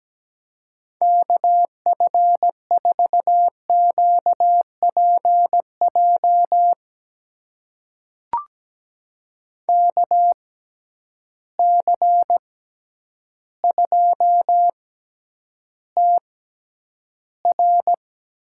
Morse code, heard as a keyed tone, about 17 wpm, spaced as for 9 wpm, 700 Hz. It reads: KF4QPJ KC2TR
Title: USA Call Signs - 6 Characters - 6x Character Spacing - 17wpm